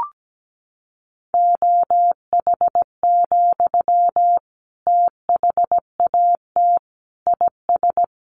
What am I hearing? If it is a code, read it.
Morse code, heard as a keyed tone, about 17 wpm, 700 Hz. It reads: OH, THAT IS
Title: Sentences from Top 1100 Words 17wpm